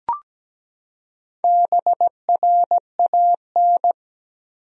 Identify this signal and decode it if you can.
Morse code, heard as a keyed tone, about 17 wpm, 700 Hz. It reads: BRAN
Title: USA Common Names - Max 4 Letters - 17wpm